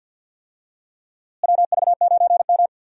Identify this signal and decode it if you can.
Morse code, heard as a keyed tone, about 46 wpm, 700 Hz. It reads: WV9M
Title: USA Call Signs - 4 Characters - Repeated 2x - 50wpm